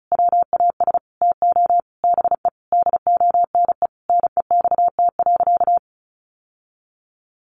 Morse code, heard as a keyed tone, about 34 wpm, 700 Hz. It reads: WAS TO BE DONE NEXT.
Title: Sentences from Top 1200 Words 35wpm